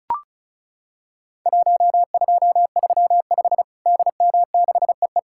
Morse code, heard as a keyed tone, about 36 wpm, 700 Hz. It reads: 1235 DM6EE
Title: Worked All Europe DX Contest - QTC Exchange - 35wpm